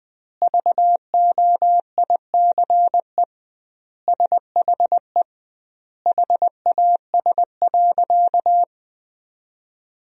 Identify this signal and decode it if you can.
Morse code, heard as a keyed tone, about 20 wpm, 700 Hz. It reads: VOICE SHE HAS.
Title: 2x Word-Spacing - Sentences from Top 500 Words - 20wpm